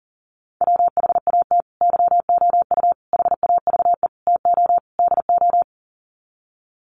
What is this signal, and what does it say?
Morse code, heard as a keyed tone, about 40 wpm, 700 Hz. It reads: WHAT YOU HAVE TO DO
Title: Sets of 5 Words Encore 40wpm